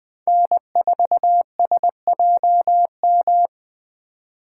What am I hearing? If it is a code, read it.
Morse code, heard as a keyed tone, about 25 wpm, 700 Hz. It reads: N4SJM
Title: Speed Racing - USA Call Signs 20wpm